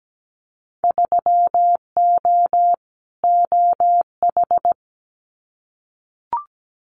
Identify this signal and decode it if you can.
Morse code, heard as a keyed tone, about 17 wpm, 700 Hz. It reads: 3O OH
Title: Winter Field Day Exchanges 17wpm